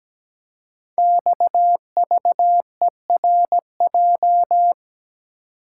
Morse code, heard as a keyed tone, about 17 wpm, 700 Hz. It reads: XVERJ
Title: Code Groups - 5 Letters - 17wpm